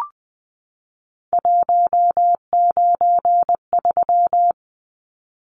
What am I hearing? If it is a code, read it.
Morse code, heard as a keyed tone, about 20 wpm, 700 Hz. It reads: 193
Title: Numbers - 3-Digit 20wpm